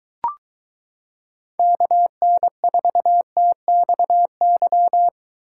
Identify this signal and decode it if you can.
Morse code, heard as a keyed tone, about 22 wpm, 700 Hz. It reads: KN4TXY